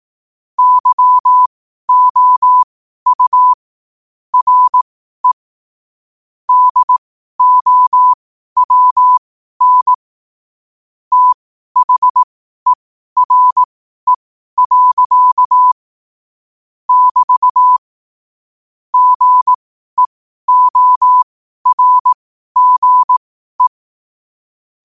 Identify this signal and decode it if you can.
Morse code, heard as a keyed tone, about 18 wpm, spaced as for 12 wpm, 1000 Hz. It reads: YOU RE DOWN THERE. <BT> GEORGE
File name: qotdfiles / 12wpm